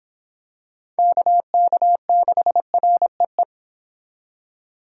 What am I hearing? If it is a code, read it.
Morse code, heard as a keyed tone, about 26 wpm, 700 Hz. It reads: KK6REE